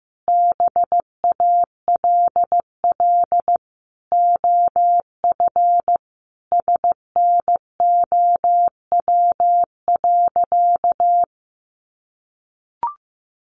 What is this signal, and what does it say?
Morse code, heard as a keyed tone, about 15 wpm, 700 Hz. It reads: BALL OF SNOW.
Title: Easy Phrases 15wpm